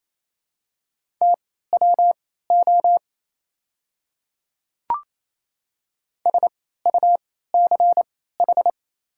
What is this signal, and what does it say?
Morse code, heard as a keyed tone, about 28 wpm, spaced as for 16 wpm, 700 Hz. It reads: TWO SUCH
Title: 3x Character Spacing - Top 200 Words - 28wpm